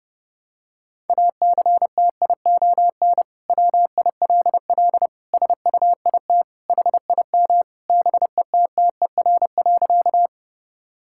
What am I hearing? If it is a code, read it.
Morse code, heard as a keyed tone, about 30 wpm, 700 Hz. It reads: ACTION WILL SUIT HIM BETTER.
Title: Sentences from Top 1600 Words 30wpm